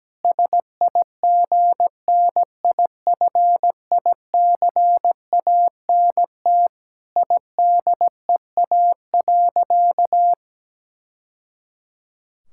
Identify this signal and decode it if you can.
Morse code, heard as a keyed tone, about 17 wpm, 700 Hz. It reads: SIGNIFICANT IDEA.
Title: Sentences from Top 1000 Words - Review Part 6 - 17wpm